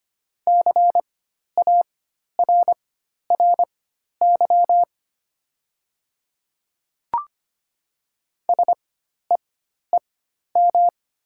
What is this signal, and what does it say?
Morse code, heard as a keyed tone, about 25 wpm, spaced as for 12 wpm, 700 Hz. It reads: CARRY SEEM
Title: 4x Character Spacing - Top 300 Words - 25wpm